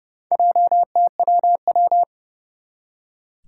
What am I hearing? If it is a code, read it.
Morse code, heard as a keyed tone, about 30 wpm, 690 Hz. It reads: JTWW
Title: Four Challenging Letter-Number Combinations 30wpm